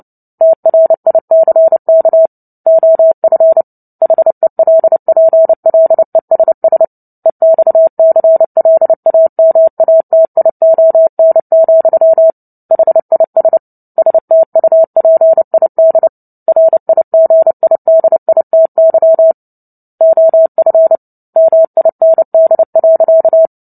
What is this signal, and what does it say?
Morse code, heard as a keyed tone, about 30 wpm, 650 Hz. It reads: TRICK OF HELPLESS EXCLAMATION, HIS STUPID RIGIDITY OF MIND.
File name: Book Two / 03-Chapter 3 - 29 WPM